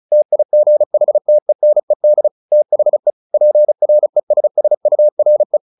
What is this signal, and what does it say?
Morse code, heard as a keyed tone, about 37 wpm, 600 Hz. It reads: TIGHTENED THE PRESSURE
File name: the old man and the sea / the old man and the sea 1 35wpm